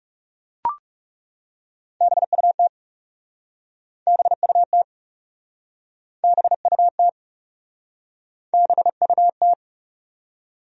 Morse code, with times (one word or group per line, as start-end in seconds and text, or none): 2.0-2.7 BUT
4.1-4.8 BUT
6.2-7.1 BUT
8.5-9.5 BUT